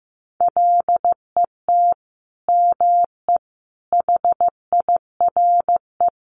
Morse code, heard as a keyed tone, about 15 wpm, 700 Hz. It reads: LET ME HIRE
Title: Sentences from Top 1400 Words 15wpm